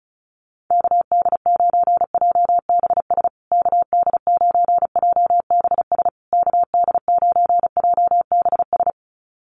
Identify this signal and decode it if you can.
Morse code, heard as a keyed tone, about 34 wpm, 700 Hz. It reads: KD9JBS KD9JBS KD9JBS
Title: USA Call Signs Repeated 3x 35wpm